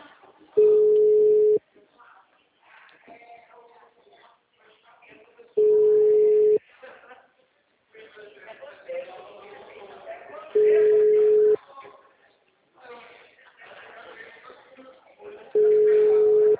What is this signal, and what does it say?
Ringback tone (Europe, 425 Hz), with speech.
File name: out- -2013- - - .692